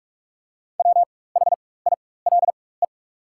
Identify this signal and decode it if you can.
Morse code, heard as a keyed tone, about 44 wpm, spaced as for 21 wpm, 690 Hz. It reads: WHILE
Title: 4x Character Spacing - Top 500 Words - 45wpm